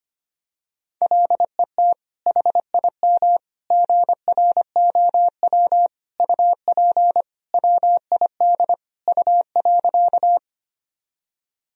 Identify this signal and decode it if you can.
Morse code, heard as a keyed tone, about 25 wpm, 700 Hz. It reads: LET HIM GROW UP WID U.
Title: Sentences from Top 300 Words and Abbreviations 25wpm